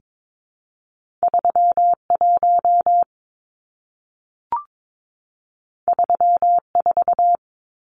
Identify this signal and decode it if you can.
Morse code, heard as a keyed tone, about 22 wpm, 700 Hz. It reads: 31 34
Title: Numbers - 2-Digit 22wpm